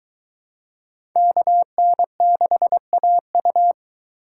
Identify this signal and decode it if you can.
Morse code, heard as a keyed tone, about 22 wpm, 700 Hz. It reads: KN6AU